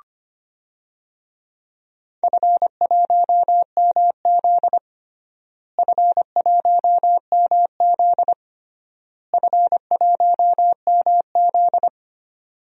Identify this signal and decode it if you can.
Morse code, heard as a keyed tone, about 25 wpm, 700 Hz. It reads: F1MZ F1MZ F1MZ
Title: DXCC-100 Call Signs - 4 Characters - Repeated 3x - 25wpm